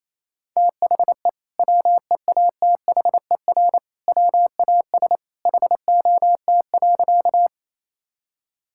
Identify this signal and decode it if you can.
Morse code, heard as a keyed tone, about 28 wpm, 700 Hz. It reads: THE WEATHER WAS HOT.
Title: Sentences from Top 1000 Words - Review Part 5 - 28wpm